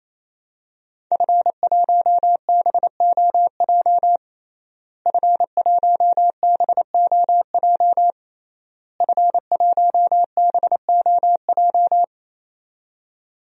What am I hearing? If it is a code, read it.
Morse code, heard as a keyed tone, about 28 wpm, 700 Hz. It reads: F1BOJ F1BOJ F1BOJ
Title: DXCC-100 Call Signs - 5 Characters - Repeated 3x - 28wpm